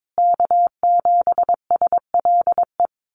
Morse code, heard as a keyed tone, about 22 wpm, 700 Hz. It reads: K7SLE
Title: USA Call Signs Repeated 3x 22wpm